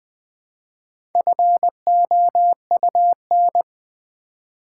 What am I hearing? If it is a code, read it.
Morse code, heard as a keyed tone, about 20 wpm, 700 Hz. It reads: FOUN
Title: Build A Word - 1 Letter At A Time - 5-Letter Words - 20wpm